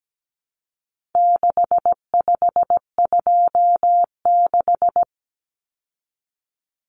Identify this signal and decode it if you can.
Morse code, heard as a keyed tone, about 17 wpm, 700 Hz. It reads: 6526
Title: Numbers - 4-Digit 17wpm